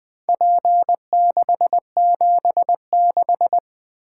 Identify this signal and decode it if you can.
Morse code, heard as a keyed tone, about 20 wpm, 700 Hz. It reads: P676
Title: Four Letter-Number Combinations - 20wpm